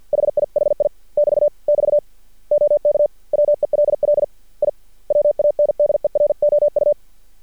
Morse code, heard as a keyed tone, about 48 wpm, 590 Hz. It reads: HIHI <BT> <BT> OK WELL I WANDEROU
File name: archived / CRSnet